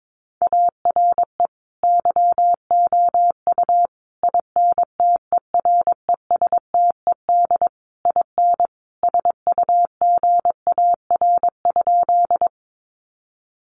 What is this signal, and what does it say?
Morse code, heard as a keyed tone, about 22 wpm, 700 Hz. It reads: ARE YOU INTERESTED IN SUGAR?
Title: Sentences from Top 1200 Words 22wpm